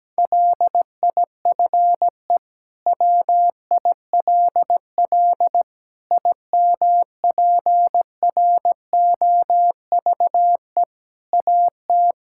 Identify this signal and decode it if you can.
Morse code, heard as a keyed tone, about 17 wpm, 700 Hz. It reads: LIFE WILL IMPROVE AT
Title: Sentences from Top 1000 Words - Review Part 5 - 17wpm